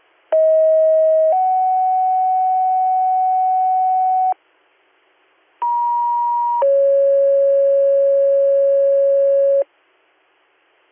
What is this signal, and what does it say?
Two-tone page, heard as a one-second tone, then a three-second tone, about 640 then 750 Hz, about 950 then 550 Hz. It is heard from 0.3 to 9.6 s.